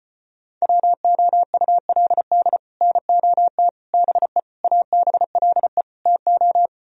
Morse code, heard as a keyed tone, about 33 wpm, 700 Hz. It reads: WOULD NOT BE ABLE TO